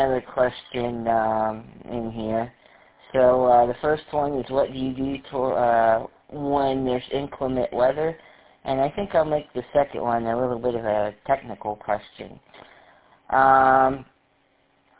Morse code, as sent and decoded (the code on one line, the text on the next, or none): none